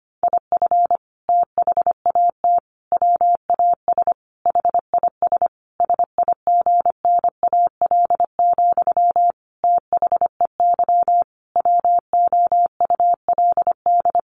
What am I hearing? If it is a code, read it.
Morse code, heard as a keyed tone, about 25 wpm, 700 Hz. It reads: IF THAT WAS HIS SIGNAL, THEY WOULD
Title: Sentences from Top 1400 Words 25wpm